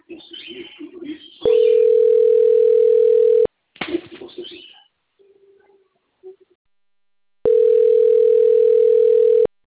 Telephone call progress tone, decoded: ringback tone

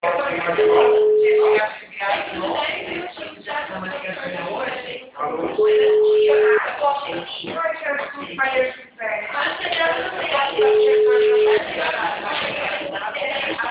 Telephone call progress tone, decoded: ringback tone

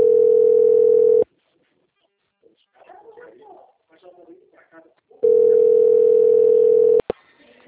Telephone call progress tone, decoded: ringback tone

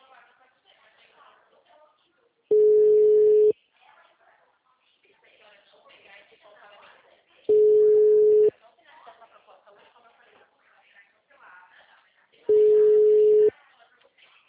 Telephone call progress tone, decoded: ringback tone

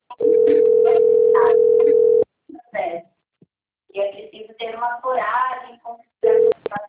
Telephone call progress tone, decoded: ringback tone

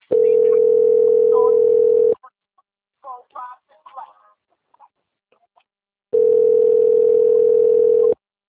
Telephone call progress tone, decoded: ringback tone

